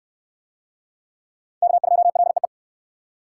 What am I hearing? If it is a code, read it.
Morse code, heard as a keyed tone, about 64 wpm, 700 Hz. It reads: K2LEE